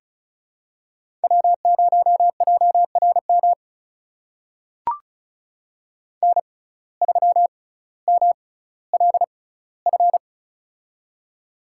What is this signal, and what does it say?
Morse code, heard as a keyed tone, about 34 wpm, spaced as for 18 wpm, 700 Hz. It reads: W0JRM N3MLF
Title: USA Call Signs - 5 Characters - 6x Character Spacing - 35wpm